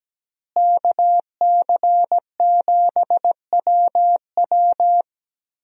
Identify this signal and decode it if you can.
Morse code, heard as a keyed tone, about 17 wpm, 700 Hz. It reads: KC7WW
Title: USA Call Signs - 5 Characters - Repeated 3x - 17wpm